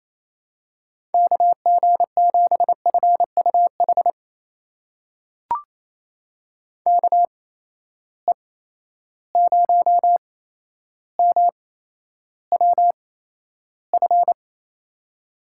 Morse code, heard as a keyed tone, about 28 wpm, spaced as for 12 wpm, 700 Hz. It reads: KG7FUH KE0MWF
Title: USA Call Signs - 6 Characters - 8x Character Spacing - 28wpm